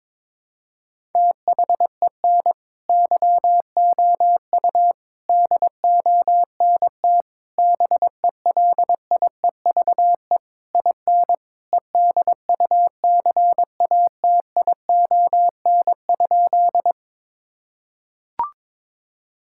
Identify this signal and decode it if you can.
Morse code, heard as a keyed tone, about 22 wpm, 700 Hz. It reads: THEN YOU DONT BELIEVE IN EDUCATION?
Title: Sentences from Top 1000 Words - Review Part 6 - 22wpm